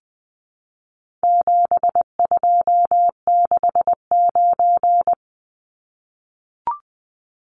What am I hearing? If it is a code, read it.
Morse code, heard as a keyed tone, about 20 wpm, 700 Hz. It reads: 7269
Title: Numbers - 4-Digit 20wpm